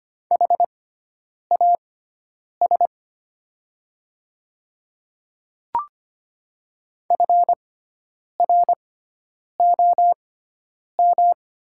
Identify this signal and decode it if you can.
Morse code, heard as a keyed tone, about 25 wpm, spaced as for 9 wpm, 700 Hz. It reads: HAS FROM
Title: 6x Character Spacing - Top 200 Words - 25wpm